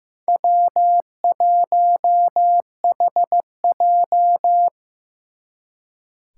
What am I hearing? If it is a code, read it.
Morse code, heard as a keyed tone, about 18 wpm, 700 Hz. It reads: W1HJ
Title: Speed Racing - USA 1x2 Call Signs 15wpm